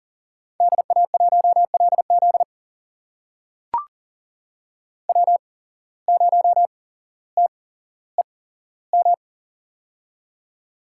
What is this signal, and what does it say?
Morse code, heard as a keyed tone, about 40 wpm, spaced as for 17 wpm, 700 Hz. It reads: DA1LZ W0TEM
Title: USA Call Signs - 8x Character Spacing - 40wpm